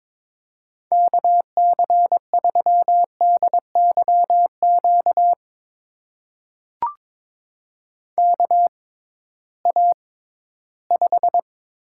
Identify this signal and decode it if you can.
Morse code, heard as a keyed tone, about 22 wpm, spaced as for 14 wpm, 700 Hz. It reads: KC3DYQ KA5
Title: USA Call Signs - 6 Characters - 6x Character Spacing - 22wpm